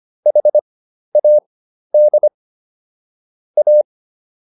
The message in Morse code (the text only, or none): HAD A